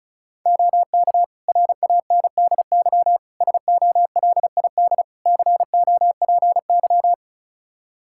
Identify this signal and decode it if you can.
Morse code, heard as a keyed tone, about 34 wpm, 700 Hz. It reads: OK RANDY SOLID COPY
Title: Practice QSO - Exchange 3 - More Info - 35wpm